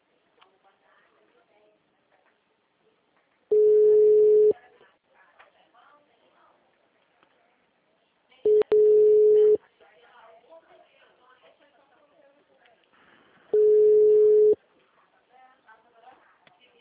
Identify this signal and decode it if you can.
Ringback tone (Europe, 425 Hz), with speech.
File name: out- -2018